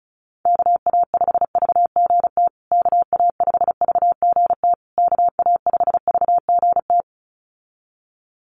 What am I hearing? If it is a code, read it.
Morse code, heard as a keyed tone, about 34 wpm, 700 Hz. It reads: KA5VGT KA5VGT KA5VGT
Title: USA Call Signs Repeated 3x 35wpm